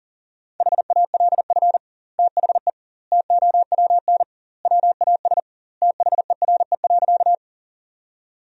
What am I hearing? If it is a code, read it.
Morse code, heard as a keyed tone, about 40 wpm, 700 Hz. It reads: HALF THE TOWN WAS THERE.
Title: 2x Word-Spacing - Sentences from Top 500 Words - 40wpm